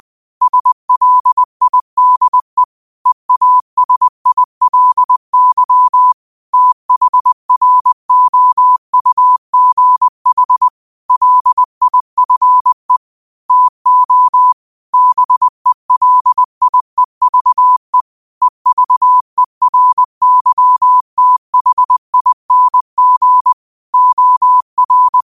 Morse code, heard as a keyed tone, about 20 wpm, 1000 Hz. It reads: SLIDE EASILY THROUGH LIFE TO BELIEVE EVERYTHING OR
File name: qotdfiles / 20wpm